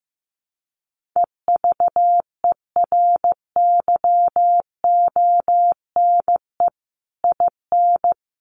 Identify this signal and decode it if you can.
Morse code, heard as a keyed tone, about 15 wpm, 700 Hz. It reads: EVERYONE IN
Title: Sentences from Top 1500 Words 15wpm